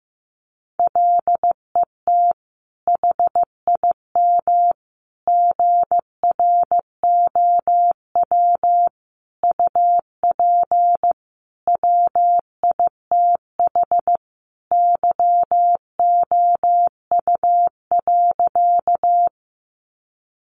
Morse code, heard as a keyed tone, about 15 wpm, 700 Hz. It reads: LET HIM GROW UP WITH YOU.
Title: Sentences from Top 300 Words 15wpm